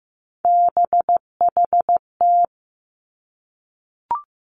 Morse code, heard as a keyed tone, about 15 wpm, 700 Hz. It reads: BHT
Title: Three Challenging Letter-Number Combinations 15wpm